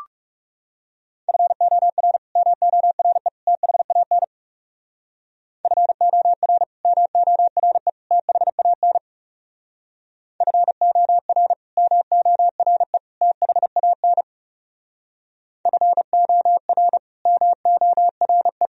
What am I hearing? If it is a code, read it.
Morse code, heard as a keyed tone, about 36 wpm, 700 Hz. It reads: FOR MORE THAN FOR MORE THAN FOR MORE THAN FOR MORE
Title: Speed-Racing - Sets of 3 Words 30wpm